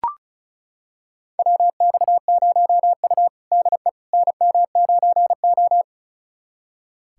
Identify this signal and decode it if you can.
Morse code, heard as a keyed tone, about 34 wpm, 700 Hz. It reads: WX0U DE NM9O
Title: USA Callsign DE Callsign - 4 Characters - 35wpm